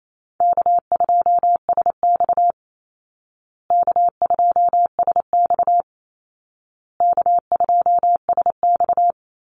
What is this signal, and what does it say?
Morse code, heard as a keyed tone, about 28 wpm, 700 Hz. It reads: K2SX K2SX K2SX
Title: Call Signs Repeated 2x - 28wpm